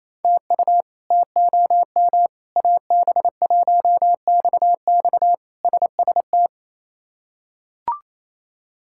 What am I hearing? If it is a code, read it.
Morse code, heard as a keyed tone, about 28 wpm, 700 Hz. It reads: TU TOM AB1XX SST
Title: K1USN SST QSO Practice - 28wpm